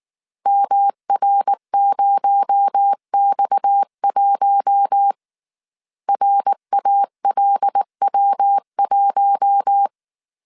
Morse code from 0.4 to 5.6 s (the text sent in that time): MR0X1